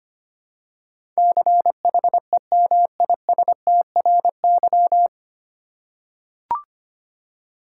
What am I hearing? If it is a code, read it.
Morse code, heard as a keyed tone, about 25 wpm, 700 Hz. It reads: CHEMISTRY
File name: practice / 25wpm / 9 Letter Words - First 3 Letters Given - 25wpm